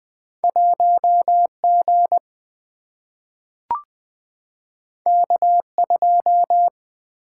Morse code, heard as a keyed tone, about 20 wpm, 700 Hz. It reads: 1 G K 2
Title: Single-Letter Single-Number 20wpm